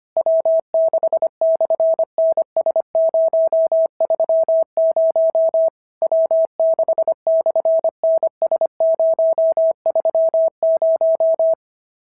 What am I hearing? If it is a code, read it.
Morse code, heard as a keyed tone, about 24 wpm, 650 Hz. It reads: W6/NS030 W6/NS030
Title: SOTA QSO Practice 25wpm